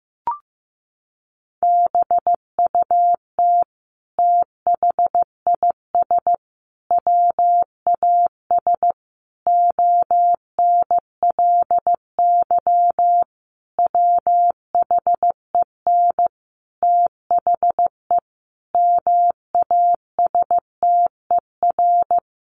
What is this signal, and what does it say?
Morse code, heard as a keyed tone, about 15 wpm, 700 Hz. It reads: BUT THIS WAS ONLY WHEN THE MASTER
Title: Sentences from Top 1300 Words 15wpm